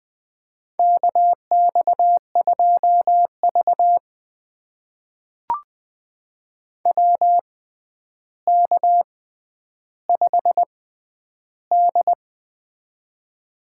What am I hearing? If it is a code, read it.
Morse code, heard as a keyed tone, about 20 wpm, spaced as for 10 wpm, 700 Hz. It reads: KX2V WK5D
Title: USA Call Signs - 4 Characters - 6x Character Spacing - 20wpm